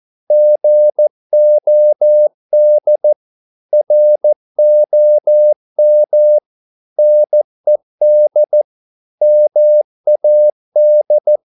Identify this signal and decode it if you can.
Morse code, heard as a keyed tone, about 14 wpm, 600 Hz. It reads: GOD ROM NED MAD
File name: Korte ord DK 14wpm